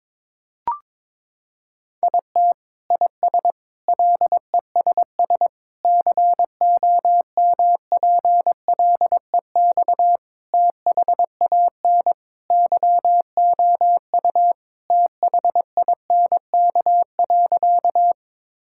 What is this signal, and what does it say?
Morse code, heard as a keyed tone, about 22 wpm, 700 Hz. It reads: IT IS LESS COMPLEX THAN YOU THINK.
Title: Sentences from Top 1000 Words - Review Part 6 - 22wpm